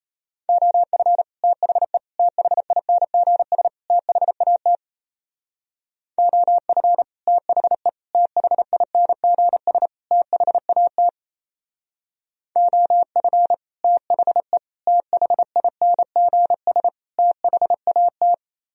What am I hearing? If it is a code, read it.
Morse code, heard as a keyed tone, about 34 wpm, 700 Hz. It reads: OF THE THINGS THAT OF THE THINGS THAT OF THE THINGS THAT
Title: Speed-Racing - Sets of 4 Words 28wpm